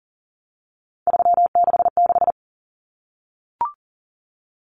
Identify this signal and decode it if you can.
Morse code, heard as a keyed tone, about 37 wpm, 700 Hz. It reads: 366
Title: Common Cut Numbers - 3-Digit 40wpm